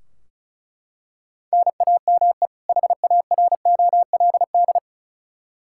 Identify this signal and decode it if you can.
Morse code, heard as a keyed tone, about 36 wpm, 700 Hz. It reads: NAME HAROLD
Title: Practice QSO - Exchange 1 - Beginning - 35wpm